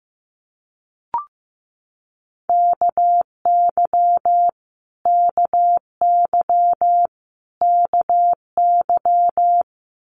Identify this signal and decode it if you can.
Morse code, heard as a keyed tone, about 15 wpm, 700 Hz. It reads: KY KY KY
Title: Multi-letter Phonemes Repeated 3x 15wpm